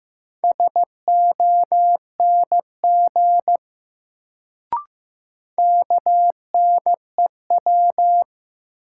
Morse code, heard as a keyed tone, about 15 wpm, 700 Hz. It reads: SONG KNEW
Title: Sentences from Top 400 Words - New Word Drill - 15wpm